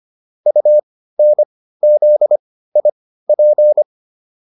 Morse code, heard as a keyed tone, about 25 wpm, spaced as for 15 wpm, 600 Hz. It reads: UNZIP